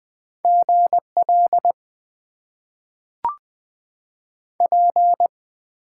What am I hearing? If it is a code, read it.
Morse code, heard as a keyed tone, about 20 wpm, 700 Hz. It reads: GL P
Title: Top 90 QSO Elements - 8x Character Spacing - 20wpm